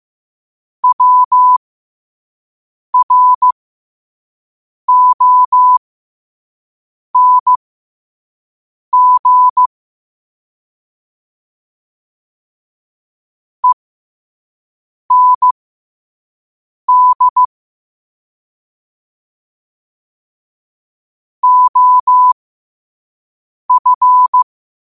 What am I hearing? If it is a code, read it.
Morse code, heard as a keyed tone, about 15 wpm, spaced as for 5 wpm, 1000 Hz. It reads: WRONG END OF
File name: 2024-10-07-5wpm